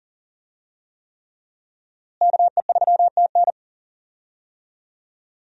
Morse code, heard as a keyed tone, about 40 wpm, 700 Hz. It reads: KE3TN
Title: USA Call Signs - 5 Characters - Repeated 3x - 40wpm